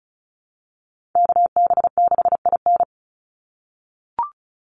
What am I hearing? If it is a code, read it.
Morse code, heard as a keyed tone, about 34 wpm, 700 Hz. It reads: KB6IN
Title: USA Call Signs 35wpm